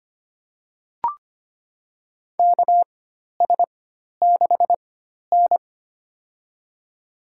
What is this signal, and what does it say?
Morse code, heard as a keyed tone, about 25 wpm, spaced as for 12 wpm, 690 Hz. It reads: KS6N
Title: USA Call Signs - 4 Characters - 4x Character Spacing - 25wpm